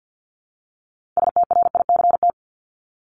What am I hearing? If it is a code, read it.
Morse code, heard as a keyed tone, about 48 wpm, 700 Hz. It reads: STRICT
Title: Top 1501-1750 Words 50wpm